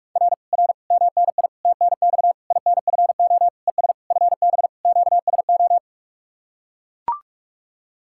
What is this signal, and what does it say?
Morse code, heard as a keyed tone, about 44 wpm, 700 Hz. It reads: R R MNI TNX INFO ES FB QSO